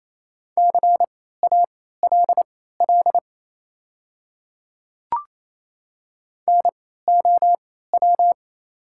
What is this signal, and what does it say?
Morse code, heard as a keyed tone, about 28 wpm, spaced as for 16 wpm, 690 Hz. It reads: CALL NOW
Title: 3x Character Spacing - Top 100 Words - 28wpm